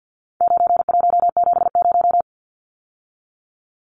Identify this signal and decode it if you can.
Morse code, heard as a keyed tone, about 50 wpm, 700 Hz. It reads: 9170
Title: Numbers - 4-Digit 50wpm